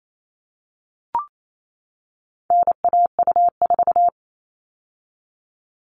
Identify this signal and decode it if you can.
Morse code, heard as a keyed tone, about 28 wpm, 700 Hz. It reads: NAU4
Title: All Cut Numbers - 4-Digit 28wpm